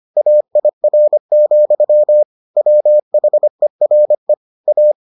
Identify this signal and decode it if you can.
Morse code, heard as a keyed tone, about 25 wpm, 600 Hz. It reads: AIR, WHERE A